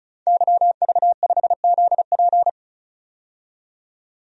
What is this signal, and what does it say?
Morse code, heard as a keyed tone, about 36 wpm, 700 Hz. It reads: YV5ZP
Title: DXCC-100 Call Signs - 5 Characters - 35wpm